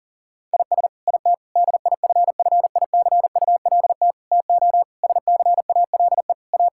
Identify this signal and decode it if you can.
Morse code, heard as a keyed tone, about 40 wpm, 700 Hz. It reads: IS IT DIFFICULT TO SCALE A
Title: Sentences from Top 1000 Words - Review Part 5 - 40wpm